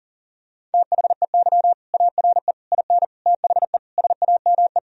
Morse code, heard as a keyed tone, about 40 wpm, 700 Hz. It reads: THEY ARE IN THE SAME
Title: Sentences from Top 1000 Words - Review Part 2 - 40wpm